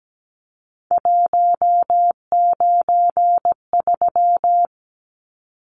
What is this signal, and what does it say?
Morse code, heard as a keyed tone, about 17 wpm, 700 Hz. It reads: 193
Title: Numbers - 3-Digit 17wpm